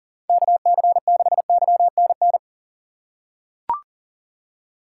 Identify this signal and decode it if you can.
Morse code, heard as a keyed tone, about 40 wpm, 700 Hz. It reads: KC6YNN
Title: USA Call Signs - 6 Characters - 4x Character Spacing - 40wpm